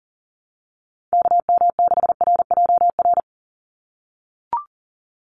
Morse code, heard as a keyed tone, about 40 wpm, 700 Hz. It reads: KM6RJR
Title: USA Call Signs 40wpm